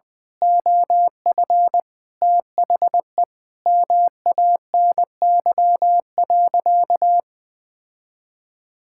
Morse code, heard as a keyed tone, about 20 wpm, 700 Hz. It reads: OF THE MANY.
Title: Sentences from Top 1000 Words - Review Part 6 - 20wpm